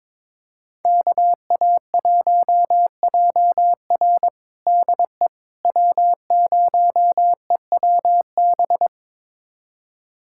Morse code, heard as a keyed tone, about 22 wpm, 700 Hz. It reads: KA1JR DE W0EWB
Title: USA Callsign DE Callsign - 5 Characters - 22wpm